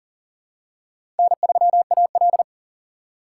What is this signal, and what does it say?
Morse code, heard as a keyed tone, about 40 wpm, 700 Hz. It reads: N3AL
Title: USA Call Signs - 2x Character Spacing - 40wpm